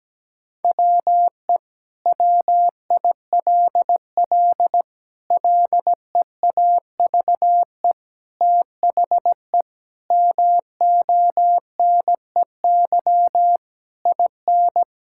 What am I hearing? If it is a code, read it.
Morse code, heard as a keyed tone, about 17 wpm, 700 Hz. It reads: WE WILL LEAVE THE MONEY IN
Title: Sentences from Top 1000 Words - Review Part 6 - 17wpm